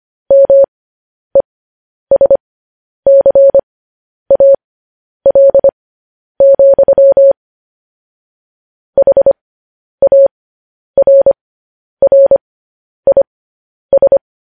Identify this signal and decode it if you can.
Morse code, heard as a keyed tone, about 25 wpm, spaced as for 10 wpm, 550 Hz. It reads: MESCAL, HARRIS
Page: news headlines in morse code at 10 wpm